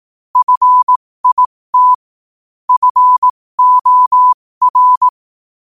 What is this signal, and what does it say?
Morse code, heard as a keyed tone, about 18 wpm, spaced as for 15 wpm, 1000 Hz. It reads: FIT FOR
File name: qotdfiles / 15wpm